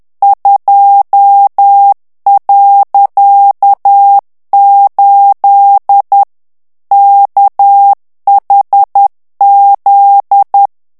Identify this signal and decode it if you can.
Morse code, heard as a keyed tone, about 10 wpm, 800 Hz. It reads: 2.8 KHZ
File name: ANGLAIS texte clair 3 vitesse 10 mots minute